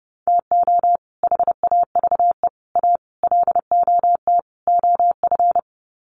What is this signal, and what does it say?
Morse code, heard as a keyed tone, about 30 wpm, 700 Hz. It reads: TO HAVE A LOT OF
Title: Sets of 5 Words Encore 30wpm